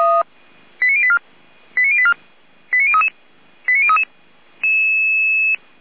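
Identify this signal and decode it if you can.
Five-tone selective call (ZVEI), heard as quick runs of five tones, tones of about 70 ms. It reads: 89084 89084 89033 89033
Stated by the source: ZVEI five-tone selective call received by SDR.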